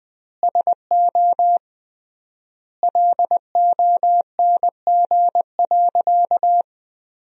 Morse code, heard as a keyed tone, about 21 wpm, 700 Hz. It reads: SO LONG.
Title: 3x Word-Spacing - Sentences from Top 100 Words - 20wpm